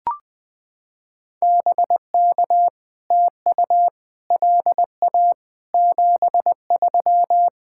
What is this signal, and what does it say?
Morse code, heard as a keyed tone, about 20 wpm, 700 Hz. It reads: BK TU LA 73
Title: POTA QSO Practice - 20wpm